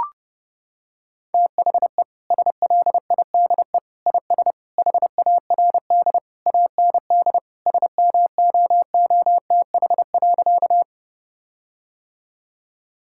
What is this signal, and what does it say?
Morse code, heard as a keyed tone, about 30 wpm, 700 Hz. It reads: THE SLIDE IS HARD AND SMOOTH.
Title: Sentences from Top 1600 Words 30wpm